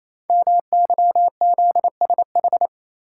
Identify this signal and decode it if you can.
Morse code, heard as a keyed tone, about 27 wpm, 700 Hz. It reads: MYZSH
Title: Code Groups - 5 Letter:Numbers - 28wpm